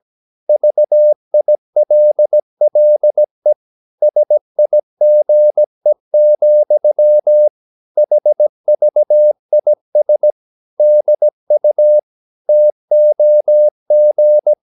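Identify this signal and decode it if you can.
Morse code, heard as a keyed tone, about 17 wpm, 600 Hz. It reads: VILLE SIGE, HVIS DU TOG
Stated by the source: Slow (QRS) Morse practice at 17 wpm, from CW med Gnister.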